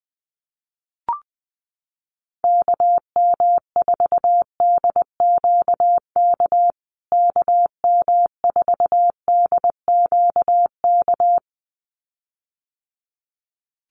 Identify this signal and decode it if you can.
Morse code, heard as a keyed tone, about 20 wpm, 700 Hz. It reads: KM4DQK KM4DQK
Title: USA Call Signs Repeated 2x 20wpm